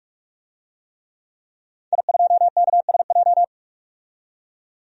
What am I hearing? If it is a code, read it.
Morse code, heard as a keyed tone, about 46 wpm, 700 Hz. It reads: I2KSJ
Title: DXCC-100 Call Signs - 5 Characters - 45wpm